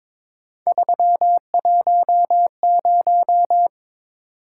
Morse code, heard as a keyed tone, about 21 wpm, 700 Hz. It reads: 310
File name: practice / 22wpm / Three Letter-Number Combinations - 22wpm